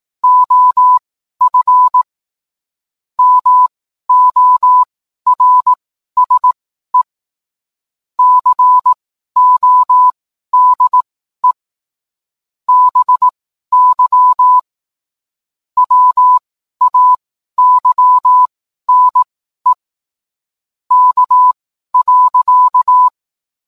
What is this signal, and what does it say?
Morse code, heard as a keyed tone, about 18 wpm, spaced as for 12 wpm, 1000 Hz. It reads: OF MORSE CODE BY WAYNE K.